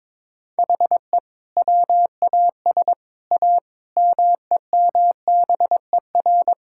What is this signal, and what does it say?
Morse code, heard as a keyed tone, about 22 wpm, 700 Hz. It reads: HE WAS A MEMBER
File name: practice / 22wpm / Sets of 6 Words - 22wpm